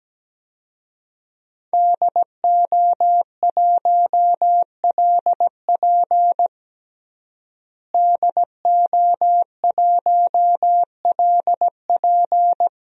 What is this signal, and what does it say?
Morse code, heard as a keyed tone, about 17 wpm, 700 Hz. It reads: DO1LP DO1LP
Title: DXCC-100 Call Signs - 5 Characters - Repeated 2x - 17wpm